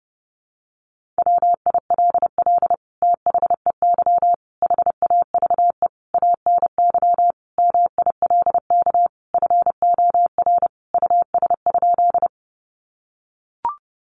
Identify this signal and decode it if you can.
Morse code, heard as a keyed tone, about 30 wpm, 700 Hz. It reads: WILL THEY HAVE ANY MILK FOR US?
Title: Sentences from Top 1500 Words 30wpm